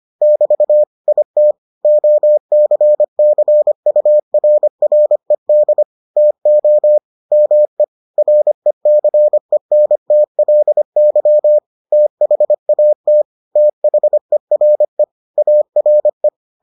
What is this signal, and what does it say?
Morse code, heard as a keyed tone, about 25 wpm, 600 Hz. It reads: <BT> IT OCCURRED TO ME RECENTLY THAT THERE ARE